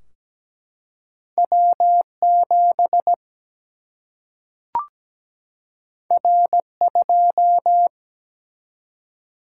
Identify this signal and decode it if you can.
Morse code, heard as a keyed tone, about 17 wpm, 700 Hz. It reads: W7 R2
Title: Single-Letter Single-Number 17wpm